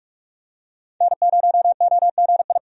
Morse code, heard as a keyed tone, about 41 wpm, 700 Hz. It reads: N0OGI
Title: USA Call Signs - 5 Characters - 45wpm